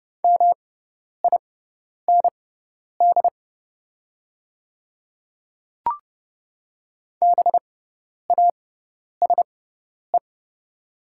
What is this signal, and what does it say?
Morse code, heard as a keyed tone, about 31 wpm, spaced as for 10 wpm, 700 Hz. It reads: MIND BASE
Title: 6x Character Spacing - Top 500 Words - 30wpm